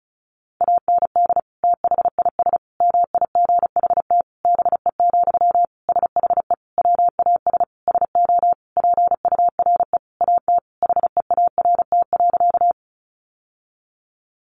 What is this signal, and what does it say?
Morse code, heard as a keyed tone, about 34 wpm, 700 Hz. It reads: AND THIS MIGHT BE, SHE WAS SO PURE AT HEART.
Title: Sentences from Top 1500 Words 35wpm